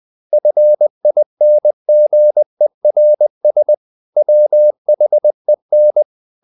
Morse code, heard as a keyed tone, about 20 wpm, 600 Hz. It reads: FINGERS WHEN
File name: the old man and the sea part 2 20wpm